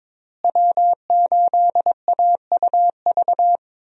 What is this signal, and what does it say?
Morse code, heard as a keyed tone, about 21 wpm, 700 Hz. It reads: W8AUV
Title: USA Call Signs - 5 Characters - Repeated 2x - 22wpm